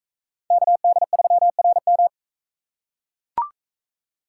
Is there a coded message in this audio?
Yes: Morse code from 0.5 to 2.1 s, heard as a keyed tone, about 41 wpm, 700 Hz. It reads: KD3RM